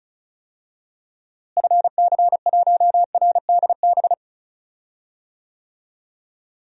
Morse code, heard as a keyed tone, about 36 wpm, 700 Hz. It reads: FC1RDB
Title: DXCC-100 Call Signs - 6 Characters - Repeated 3x - 35wpm